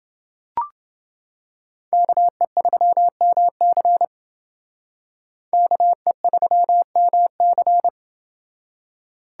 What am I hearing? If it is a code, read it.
Morse code, heard as a keyed tone, about 28 wpm, 700 Hz. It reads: KE3MC KE3MC